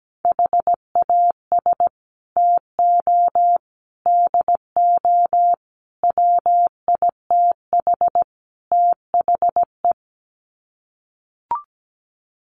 Morse code, heard as a keyed tone, about 17 wpm, 700 Hz. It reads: HAS TO DO WITH THE
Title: Sets of 5 Words 17wpm